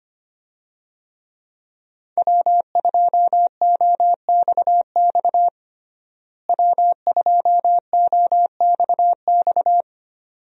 Morse code, heard as a keyed tone, about 25 wpm, 700 Hz. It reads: W2OXX W2OXX
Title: USA Call Signs - 5 Characters - Repeated 2x - 25wpm